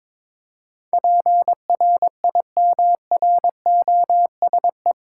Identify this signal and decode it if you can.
Morse code, heard as a keyed tone, about 21 wpm, 700 Hz. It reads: PRIMROSE
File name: Difficult Words with Hints - 22wpm